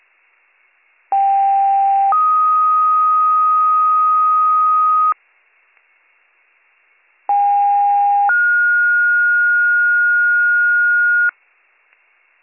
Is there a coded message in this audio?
Yes: two-tone page from 1.1 to 11.3 s, heard as a one-second tone, then a three-second tone, about 790 then 1250 Hz, about 810 then 1430 Hz.